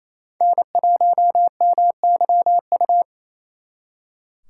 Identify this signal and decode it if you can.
Morse code, heard as a keyed tone, about 34 wpm, 700 Hz. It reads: N1MYU
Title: Speed Racing - USA 1x3 Call Signs 28wpm